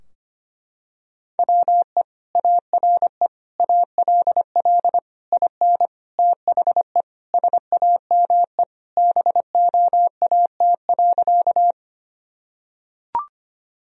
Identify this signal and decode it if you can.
Morse code, heard as a keyed tone, about 25 wpm, 700 Hz. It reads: WE ARE ALL IN THE SAME BOAT.
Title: Sentences from Top 1000 Words - Review Part 3 - 25wpm